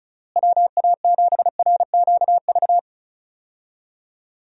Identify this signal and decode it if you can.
Morse code, heard as a keyed tone, about 36 wpm, 700 Hz. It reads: WA7RQV
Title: USA Call Signs - 6 Characters - Repeated 3x - 35wpm